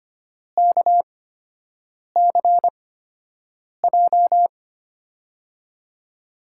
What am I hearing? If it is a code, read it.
Morse code, heard as a keyed tone, about 25 wpm, spaced as for 7 wpm, 700 Hz. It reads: KCJ
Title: USA Call Signs - 6 Characters - 8x Character Spacing - 25wpm